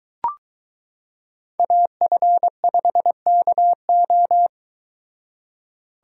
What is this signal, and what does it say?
Morse code, heard as a keyed tone, about 23 wpm, 700 Hz. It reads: AF5KO